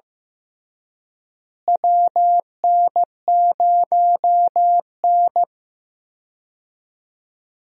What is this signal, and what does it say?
Morse code, heard as a keyed tone, about 15 wpm, 700 Hz. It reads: WN0N
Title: USA Call Signs - 4 Characters - Repeated 2x - 15wpm